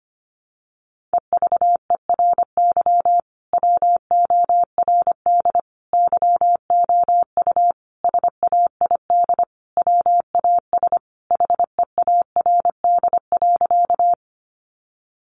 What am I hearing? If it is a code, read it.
Morse code, heard as a keyed tone, about 25 wpm, 700 Hz. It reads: EVERY WORD YOU SAID WAS HEARD.
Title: Sentences from Top 400 Words 25wpm